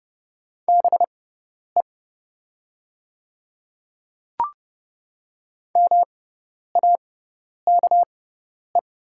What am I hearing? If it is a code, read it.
Morse code, heard as a keyed tone, about 30 wpm, spaced as for 10 wpm, 700 Hz. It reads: BE MAKE
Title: 6x Character Spacing - Top 300 Words - 30wpm